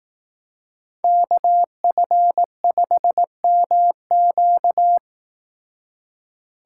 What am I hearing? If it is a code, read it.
Morse code, heard as a keyed tone, about 18 wpm, 700 Hz. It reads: KF5MQ
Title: Speed Racing - USA 2x2 Call Signs 15wpm